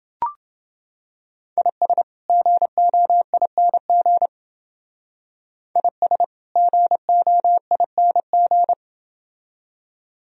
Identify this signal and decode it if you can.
Morse code, heard as a keyed tone, about 27 wpm, 700 Hz. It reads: IS GOING IS GOING